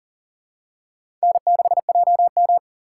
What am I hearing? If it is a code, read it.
Morse code, heard as a keyed tone, about 40 wpm, 700 Hz. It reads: N6JM
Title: USA Call Signs - 6x Character Spacing - 40wpm